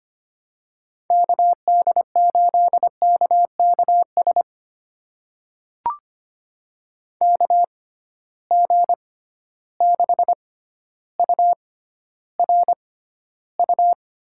Morse code, heard as a keyed tone, about 25 wpm, spaced as for 13 wpm, 700 Hz. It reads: KD8KKS KG6URU
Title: USA Call Signs - 6 Characters - 6x Character Spacing - 25wpm